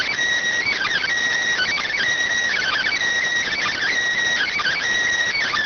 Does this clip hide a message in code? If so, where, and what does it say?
Yes: Emergency Alert System (SAME) tones from 0.7 to 4.8 s, heard as harsh buzzing bursts.